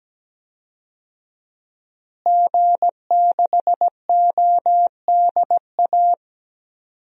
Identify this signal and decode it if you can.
Morse code, heard as a keyed tone, about 17 wpm, 700 Hz. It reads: G6ODA
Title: DXCC-100 Call Signs - 5 Characters - Repeated 2x - 17wpm